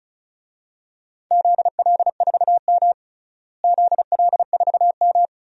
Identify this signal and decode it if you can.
Morse code, heard as a keyed tone, about 36 wpm, 700 Hz. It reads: ZL4M ZL4M
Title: DXCC-100 Call Signs - 4 Characters - Repeated 2x - 35wpm